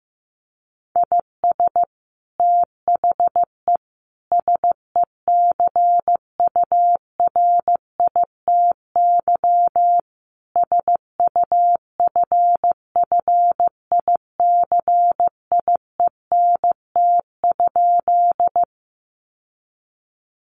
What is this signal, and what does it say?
Morse code, heard as a keyed tone, about 15 wpm, 700 Hz. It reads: IS THE SECURITY SUFFICIENT?
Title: Sentences from Top 1400 Words 15wpm